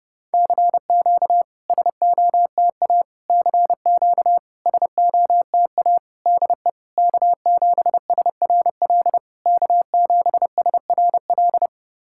Morse code, heard as a keyed tone, about 30 wpm, 660 Hz. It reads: CQ SOTA CQ SOTA DE K7SRL K7SRL
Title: SOTA QSO Practice 30wpm